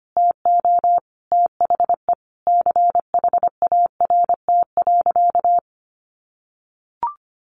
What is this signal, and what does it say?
Morse code, heard as a keyed tone, about 25 wpm, 700 Hz. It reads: TO THE CHART.
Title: Sentences from Top 1300 Words 25wpm